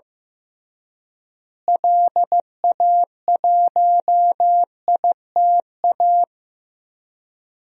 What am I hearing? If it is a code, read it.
Morse code, heard as a keyed tone, about 15 wpm, 700 Hz. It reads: LA1ITA
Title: DXCC-100 Call Signs - 6 Characters - Repeated 3x - 15wpm